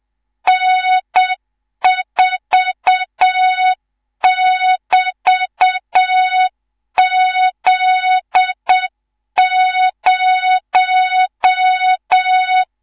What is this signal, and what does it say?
Morse code, heard as a keyed tone, about 7 wpm, 3700 Hz, with noise. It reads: N4<BT>Z0
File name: Morze-HA2MN / hang / 19 GYAKORLAT